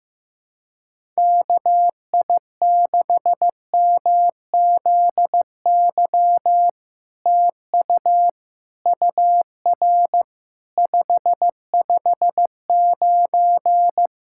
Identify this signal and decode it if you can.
Morse code, heard as a keyed tone, about 15 wpm, 700 Hz. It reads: KI6MZY TU UR 559
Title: SOTA QSO Practice 15wpm